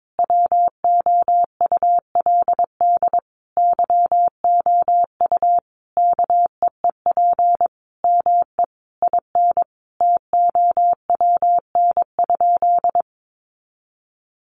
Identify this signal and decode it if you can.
Morse code, heard as a keyed tone, about 22 wpm, 700 Hz. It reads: WOULD YOU KEEP ME IN TOWN?
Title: Sentences from Top 500 Words 22wpm